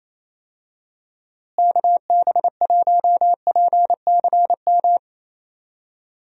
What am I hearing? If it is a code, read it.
Morse code, heard as a keyed tone, about 27 wpm, 700 Hz. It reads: KB1PCM
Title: USA Call Signs - 6 Characters - Repeated 3x - 28wpm